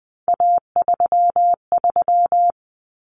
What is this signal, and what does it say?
Morse code, heard as a keyed tone, about 20 wpm, 700 Hz. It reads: A33
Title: Common Cut Numbers - 3-Digit 20wpm